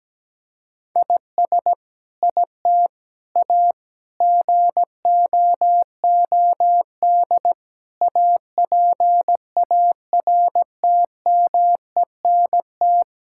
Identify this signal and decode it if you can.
Morse code, heard as a keyed tone, about 17 wpm, 700 Hz. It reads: IS IT A GOOD APARTMENT
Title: Sentences from Top 1000 Words - Review Part 6 - 17wpm